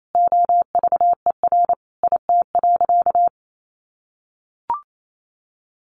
Morse code, heard as a keyed tone, about 28 wpm, 700 Hz. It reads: OVER IT.
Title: Sentences from Top 1500 Words 28wpm